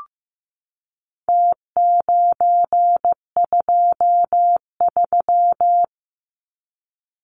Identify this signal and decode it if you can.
Morse code, heard as a keyed tone, about 15 wpm, 700 Hz. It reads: T923
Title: Numbers - Contest Style - 4-Digit 15wpm